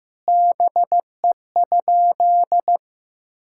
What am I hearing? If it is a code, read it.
Morse code, heard as a keyed tone, about 16 wpm, 700 Hz. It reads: BE?
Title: Sentences from Top 1000 Words - Review Part 5 - 15wpm